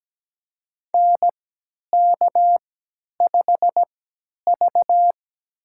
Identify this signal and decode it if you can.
Morse code, heard as a keyed tone, about 17 wpm, spaced as for 10 wpm, 700 Hz. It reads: NK5V
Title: USA Call Signs - 3x Character Spacing - 17wpm